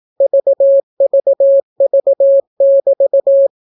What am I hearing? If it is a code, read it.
Morse code, heard as a keyed tone, about 18 wpm, 550 Hz. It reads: VVV<BT>